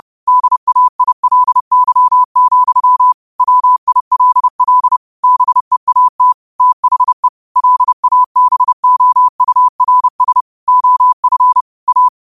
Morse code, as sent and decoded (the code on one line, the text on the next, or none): -.. .- .. .-.. -.-- --..-- / .-- .. .-.. .-.. / -... .- - / - .... / .-.. .- -... --- ..- .-. ... / --- ..-. / .-
DAILY, WILL BEAT THE LABOURS OF A